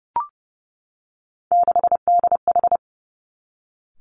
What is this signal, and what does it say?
Morse code, heard as a keyed tone, about 30 wpm, 700 Hz. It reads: 6DH
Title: Three Challenging Letter-Number Combinations 30wpm